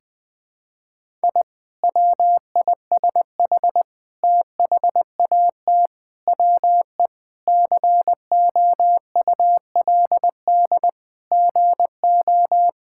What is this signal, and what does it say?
Morse code, heard as a keyed tone, about 20 wpm, 700 Hz. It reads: I WISH THAT WE COULD GO
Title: Sentences with 2-Word Phrases - Spoken Beginning - 20wpm